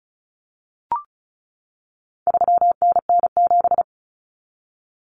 Morse code, heard as a keyed tone, about 34 wpm, 700 Hz. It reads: 3NN7
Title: Common Cut Numbers - 4-Digit 35wpm